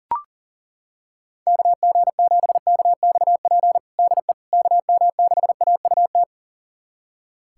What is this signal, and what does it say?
Morse code, heard as a keyed tone, about 37 wpm, 700 Hz. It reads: KG7KXP DE KM6AUT
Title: USA Callsign DE Callsign - 6 Characters - 40wpm